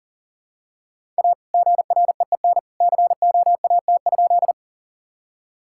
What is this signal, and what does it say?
Morse code, heard as a keyed tone, about 37 wpm, 700 Hz. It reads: A GREEN COAT?
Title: Sentences from Top 1600 Words 40wpm